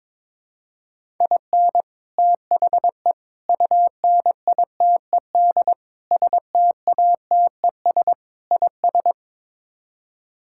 Morse code, heard as a keyed tone, about 22 wpm, 700 Hz. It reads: IN THE UNITED STATES IS
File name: practice / 22wpm / Build A Phrase - 5-Word Phrases - 22wpm